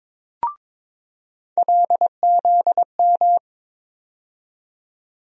Morse code, heard as a keyed tone, about 22 wpm, 700 Hz. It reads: LZM
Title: USA Call Sign Postfixes - 3 Letters 22wpm